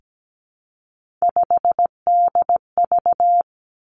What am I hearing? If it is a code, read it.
Morse code, heard as a keyed tone, about 17 wpm, 700 Hz. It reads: 5DV
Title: Three Challenging Letter-Number Combinations 17wpm